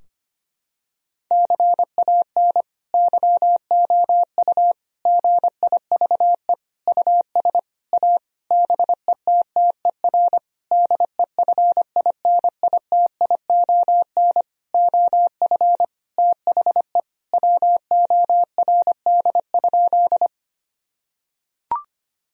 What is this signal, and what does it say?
Morse code, heard as a keyed tone, about 25 wpm, 700 Hz. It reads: CAN YOU GIVE US A BETTER DEFINITION OF THE WORD?
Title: Sentences from Top 1000 Words - Review Part 4 - 25wpm